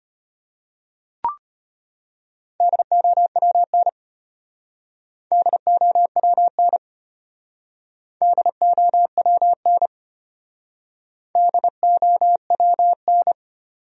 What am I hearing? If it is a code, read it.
Morse code, heard as a keyed tone, about 32 wpm, 700 Hz. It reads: DOWN DOWN DOWN DOWN